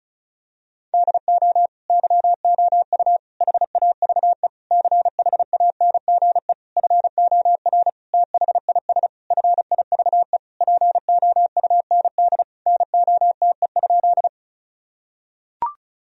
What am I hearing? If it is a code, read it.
Morse code, heard as a keyed tone, about 36 wpm, 700 Hz. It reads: DO YOU HAVE CHANGE FOR THIS FIVE POUND NOTE?
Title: Sentences from Top 1000 Words - Review Part 2 - 35wpm